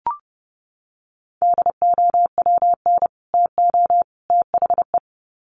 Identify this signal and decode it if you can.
Morse code, heard as a keyed tone, about 30 wpm, 700 Hz. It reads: DOWN TO THE
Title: Sets of 3 Words 30wpm